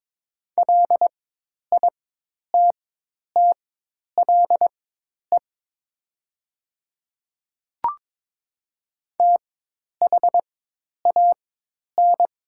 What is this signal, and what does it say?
Morse code, heard as a keyed tone, about 22 wpm, spaced as for 10 wpm, 700 Hz. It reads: LITTLE THAN
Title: 4x Character Spacing - Top 300 Words - 22wpm